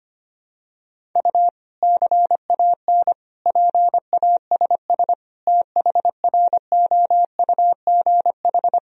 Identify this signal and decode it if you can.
Morse code, heard as a keyed tone, about 25 wpm, 700 Hz. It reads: U CAN PASS THROUGH
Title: Sentences from Top 400 Words and Abbreviations 25wpm